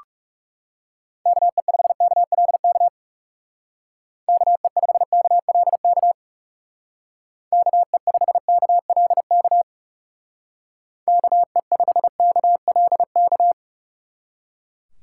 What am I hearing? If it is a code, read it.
Morse code, heard as a keyed tone, about 39 wpm, 700 Hz. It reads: KE5KLK KE5KLK KE5KLK KE5KLK